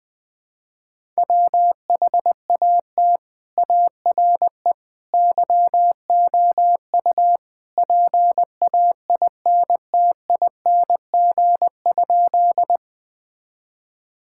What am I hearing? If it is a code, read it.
Morse code, heard as a keyed tone, about 20 wpm, 700 Hz. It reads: WHAT ARE YOU PAINTING?
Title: Sentences from Top 1000 Words - Review Part 6 - 20wpm